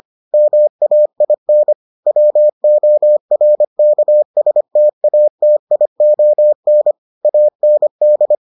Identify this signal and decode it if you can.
Morse code, heard as a keyed tone, about 25 wpm, 600 Hz. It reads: MAIN WORKSTATION AND